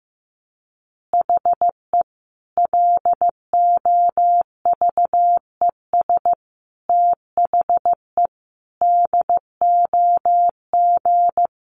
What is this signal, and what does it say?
Morse code, heard as a keyed tone, about 15 wpm, 700 Hz. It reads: HE LOVES THE DOG
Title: Additional Easy Phrases 15wpm